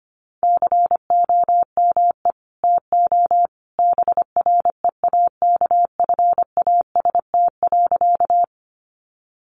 Morse code, heard as a keyed tone, about 25 wpm, 700 Hz. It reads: COME TO BREAKFAST.
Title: Sentences from Top 1500 Words 25wpm